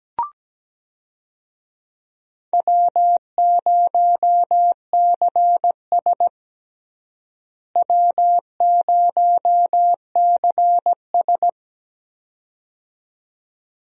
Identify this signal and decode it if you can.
Morse code, heard as a keyed tone, about 17 wpm, 700 Hz. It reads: W0CS W0CS
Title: USA Call Signs - 4 Characters - Repeated 2x - 17wpm